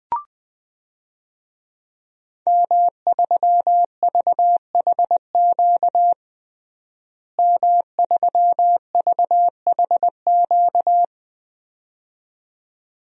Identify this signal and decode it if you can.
Morse code, heard as a keyed tone, about 21 wpm, 700 Hz. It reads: M3VHQ M3VHQ
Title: DXCC-100 Call Signs - 5 Characters - Repeated 2x - 20wpm